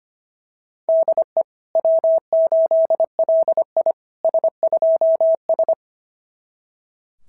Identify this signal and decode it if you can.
Morse code, heard as a keyed tone, about 25 wpm, 660 Hz. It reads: DE W8LI S2S
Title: SOTA QSO Practice 25wpm